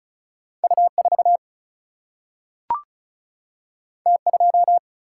Morse code, heard as a keyed tone, about 36 wpm, 700 Hz. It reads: U4 T2
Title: Single-Letter Single-Number 35wpm